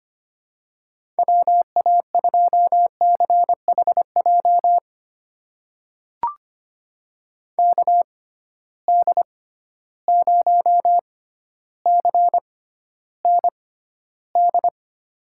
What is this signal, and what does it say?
Morse code, heard as a keyed tone, about 25 wpm, spaced as for 13 wpm, 700 Hz. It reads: WA2CHJ KD0CND
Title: USA Call Signs - 6 Characters - 6x Character Spacing - 25wpm